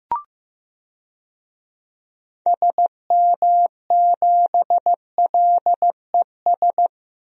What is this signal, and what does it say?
Morse code, heard as a keyed tone, about 15 wpm, 700 Hz. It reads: SM7LES
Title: DXCC-100 Call Signs - 6 Characters - Repeated 2x - 15wpm